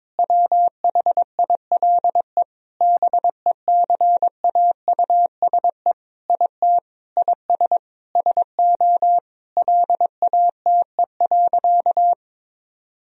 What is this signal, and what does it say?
Morse code, heard as a keyed tone, about 22 wpm, 700 Hz. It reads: WHILE BECAUSE IT IS SO LATE.
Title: Sentences from Top 1000 Words - Review Part 3 - 22wpm